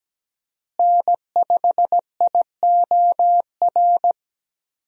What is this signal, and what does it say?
Morse code, heard as a keyed tone, about 17 wpm, 700 Hz. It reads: N5IOR
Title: USA Call Signs - 5 Characters - Repeated 3x - 17wpm